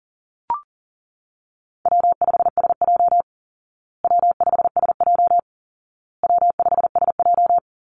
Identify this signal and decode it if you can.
Morse code, heard as a keyed tone, about 37 wpm, 700 Hz. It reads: W5SJ W5SJ W5SJ
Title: Call Signs Repeated 2x - 40wpm